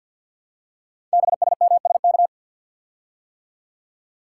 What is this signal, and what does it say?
Morse code, heard as a keyed tone, about 49 wpm, 700 Hz. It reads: BSMSK
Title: Code Groups - 5 Letters - 50wpm